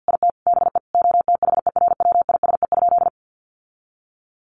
Morse code, heard as a keyed tone, about 48 wpm, 700 Hz. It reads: IT BE OTHERWISE?
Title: Sentences from Top 1200 Words 50wpm